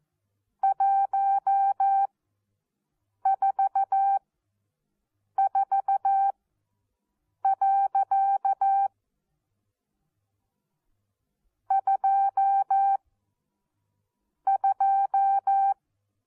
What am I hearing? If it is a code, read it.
Morse code, heard as a keyed tone, about 14 wpm, spaced as for 6 wpm, 780 Hz. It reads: 144. 22
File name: Onlinelehrgang / morsen / DJ4UF Morsen14